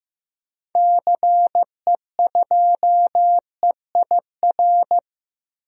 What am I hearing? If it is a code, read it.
Morse code, heard as a keyed tone, about 15 wpm, 700 Hz. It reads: CE2EIR
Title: DXCC-100 Call Signs - 6 Characters - Repeated 3x - 15wpm